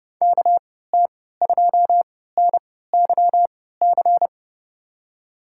Morse code, heard as a keyed tone, about 30 wpm, spaced as for 17 wpm, 700 Hz. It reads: KT2NYC
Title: USA Call Signs - 3x Character Spacing - 30wpm